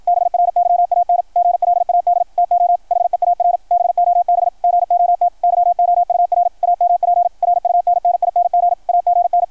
Morse code, heard as a keyed tone, about 53 wpm, 700 Hz. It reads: BRYAN GLAD TO HEAR BOB GOT YOUR AMP RUNNING AGN
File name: archived / CRSnet